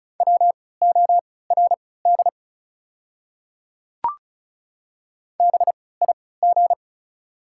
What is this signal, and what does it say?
Morse code, heard as a keyed tone, about 36 wpm, spaced as for 20 wpm, 700 Hz. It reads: WORD BIG
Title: 3x Character Spacing - Top 500 Words - 35wpm